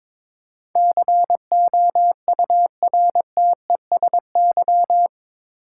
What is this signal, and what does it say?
Morse code, heard as a keyed tone, about 21 wpm, 700 Hz. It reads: COURTESY
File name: practice / 22wpm / Eight Letter Words - 22wpm